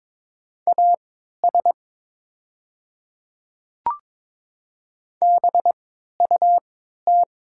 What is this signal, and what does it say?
Morse code, heard as a keyed tone, about 22 wpm, spaced as for 12 wpm, 700 Hz. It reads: AS BUT
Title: 3x Character Spacing - Top 100 Words - 22wpm